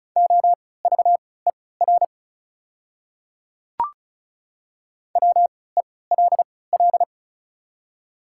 Morse code, heard as a keyed tone, about 36 wpm, spaced as for 20 wpm, 700 Hz. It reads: OVER WELL
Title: 3x Character Spacing - Top 200 Words - 35wpm